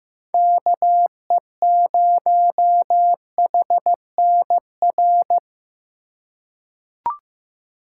Morse code, heard as a keyed tone, about 15 wpm, 700 Hz. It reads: KE0HNR
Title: USA Call Signs - 3x Character Spacing - 15wpm